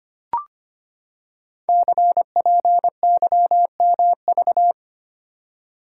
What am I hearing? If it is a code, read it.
Morse code, heard as a keyed tone, about 25 wpm, 700 Hz. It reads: CPYMV
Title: Code Groups - 5 Letter:Numbers - 25wpm